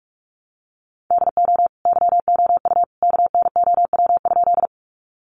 Morse code, heard as a keyed tone, about 44 wpm, 700 Hz. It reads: DO YOU KNOW?
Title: Sentences from Top 100 Words 45wpm